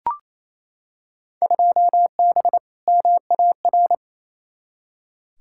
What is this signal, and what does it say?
Morse code, heard as a keyed tone, about 28 wpm, 700 Hz. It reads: 2B MAR
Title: Field Day Exchanges Encore - 28wpm